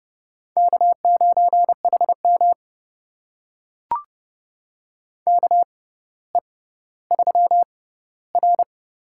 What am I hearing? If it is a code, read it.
Morse code, heard as a keyed tone, about 30 wpm, spaced as for 15 wpm, 700 Hz. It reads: K9HM KE3R
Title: USA Call Signs - 4 Characters - 6x Character Spacing - 30wpm